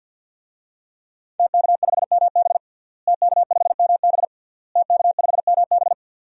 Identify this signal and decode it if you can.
Morse code, heard as a keyed tone, about 50 wpm, 700 Hz. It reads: TK5MB TK5MB TK5MB
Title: DXCC-100 Call Signs - 5 Characters - Repeated 3x - 50wpm